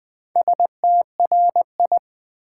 Morse code, heard as a keyed tone, about 20 wpm, 700 Hz. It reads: STRI
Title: Build A Word - 2 Letters At A Time - 6-Letter Words - 20wpm